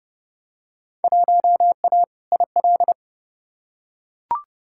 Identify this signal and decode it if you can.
Morse code, heard as a keyed tone, about 30 wpm, 700 Hz. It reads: 1A IL
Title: Field Day Exchanges - 30wpm